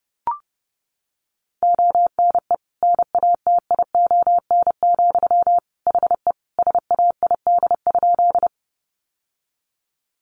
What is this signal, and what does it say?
Morse code, heard as a keyed tone, about 30 wpm, 700 Hz. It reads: ONE NATION, HE SAID?
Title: Sentences from Top 900 Words 30wpm